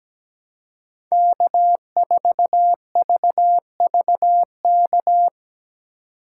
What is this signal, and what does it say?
Morse code, heard as a keyed tone, about 17 wpm, 700 Hz. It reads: K4VVK
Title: USA Call Signs - 5 Characters - 2x Character Spacing - 17wpm